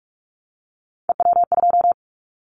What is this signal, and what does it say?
Morse code, heard as a keyed tone, about 44 wpm, 700 Hz. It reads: EW2
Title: Callsign Prefixes for Top 100 DXCC - 45wpm